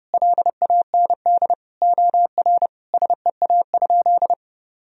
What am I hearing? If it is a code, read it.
Morse code, heard as a keyed tone, about 30 wpm, 700 Hz. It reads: LAND OR SEA?
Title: Sentences from Top 1000 Words - Review Part 1 - 30wpm